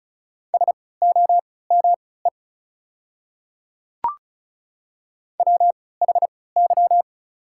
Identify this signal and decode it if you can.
Morse code, heard as a keyed tone, about 36 wpm, spaced as for 20 wpm, 700 Hz. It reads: SOME WHY
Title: 3x Character Spacing - Top 200 Words - 35wpm